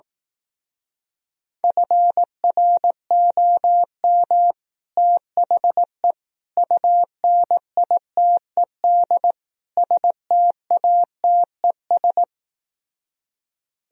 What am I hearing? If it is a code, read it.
Morse code, heard as a keyed tone, about 18 wpm, 700 Hz. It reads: FROM THE UNITED STATES
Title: Speed-Racing - Sets of 4 Words 15wpm